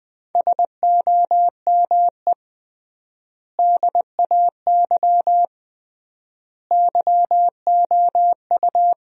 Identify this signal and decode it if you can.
Morse code, heard as a keyed tone, about 20 wpm, 700 Hz. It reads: SOME DAY YOU
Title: 3x Word-Spacing - Sentences from Top 200 Words - 20wpm